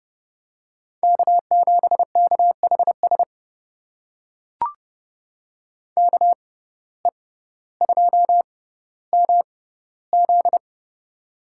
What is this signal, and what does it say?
Morse code, heard as a keyed tone, about 30 wpm, spaced as for 15 wpm, 700 Hz. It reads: K7KHS KE2MZ
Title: USA Call Signs - 5 Characters - 6x Character Spacing - 30wpm